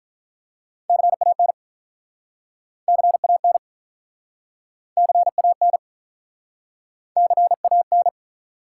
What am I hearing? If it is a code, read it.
Morse code, heard as a keyed tone, about 47 wpm, 700 Hz. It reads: CAN CAN CAN CAN